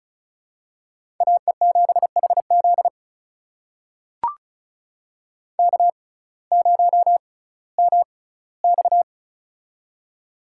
Morse code, heard as a keyed tone, about 36 wpm, spaced as for 19 wpm, 700 Hz. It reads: AE7HZ K0MX
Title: USA Call Signs - 6x Character Spacing - 35wpm